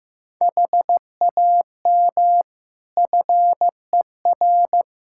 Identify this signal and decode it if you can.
Morse code, heard as a keyed tone, about 15 wpm, 700 Hz. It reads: HAM FER
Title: Practice QSO - Exchange 3 - More Info - 15wpm